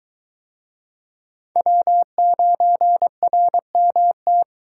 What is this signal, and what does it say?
Morse code, heard as a keyed tone, about 22 wpm, 700 Hz. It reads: W9RMT